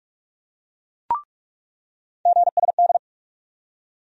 Morse code, heard as a keyed tone, about 42 wpm, 700 Hz. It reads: GSD
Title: USA Call Sign Postfixes - 3 Letters 45wpm